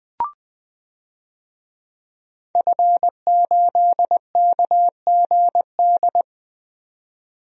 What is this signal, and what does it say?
Morse code, heard as a keyed tone, about 20 wpm, 700 Hz. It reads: F8KGD
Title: DXCC-100 Call Signs - 5 Characters - 20wpm